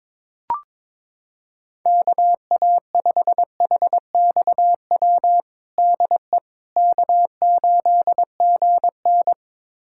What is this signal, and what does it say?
Morse code, heard as a keyed tone, about 22 wpm, 700 Hz. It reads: KA5HXW DE K8GN
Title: USA Callsign DE Callsign 22wpm